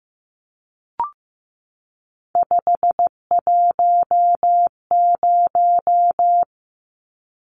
Morse code, heard as a keyed tone, about 15 wpm, 700 Hz. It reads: 510
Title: Numbers - 3-Digit 15wpm